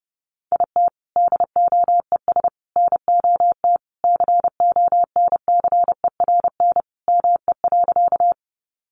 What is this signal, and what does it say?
Morse code, heard as a keyed tone, about 30 wpm, 700 Hz. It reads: IT DOES NOT CONCERN ME.
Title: Sentences from Top 1000 Words 30wpm